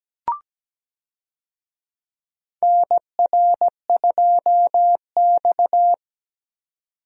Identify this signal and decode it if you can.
Morse code, heard as a keyed tone, about 17 wpm, 700 Hz. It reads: NR2X
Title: USA Call Signs - 4 Characters - 17wpm